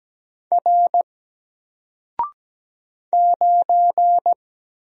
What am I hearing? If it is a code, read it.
Morse code, heard as a keyed tone, about 17 wpm, 700 Hz. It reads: R 9
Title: Single Letter-Number Alt 17wpm